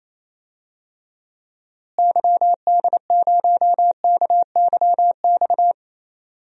Morse code, heard as a keyed tone, about 28 wpm, 700 Hz. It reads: YD0KYX
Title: DXCC-100 Call Signs - 6 Characters - Repeated 2x - 28wpm